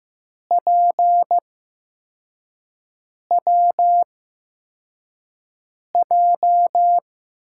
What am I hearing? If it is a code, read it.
Morse code, heard as a keyed tone, about 15 wpm, spaced as for 4 wpm, 700 Hz. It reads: PWJ
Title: USA Call Signs - 5 Characters - 8x Character Spacing - 15wpm